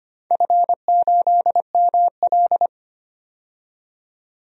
Morse code, heard as a keyed tone, about 25 wpm, 700 Hz. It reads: F8ML
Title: DXCC-100 Call Signs - 4 Characters - 25wpm